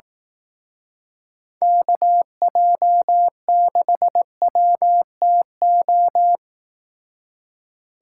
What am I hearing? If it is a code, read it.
Morse code, heard as a keyed tone, about 18 wpm, 700 Hz. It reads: KJ6WTO
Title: Speed Racing - USA Call Signs 15wpm